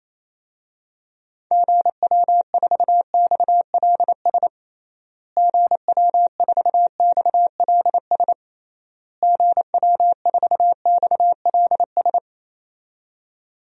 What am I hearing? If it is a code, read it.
Morse code, heard as a keyed tone, about 28 wpm, 700 Hz. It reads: GW4XLS GW4XLS GW4XLS
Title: DXCC-100 Call Signs - 6 Characters - Repeated 3x - 28wpm